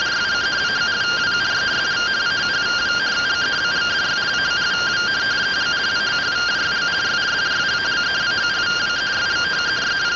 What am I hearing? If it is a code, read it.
Two-tone teleprinter signal, about 170 Hz shift.